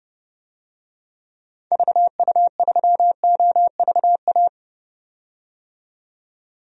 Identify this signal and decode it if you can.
Morse code, heard as a keyed tone, about 32 wpm, 700 Hz. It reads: VU3OVA
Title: DXCC-100 Call Signs - 6 Characters - 30wpm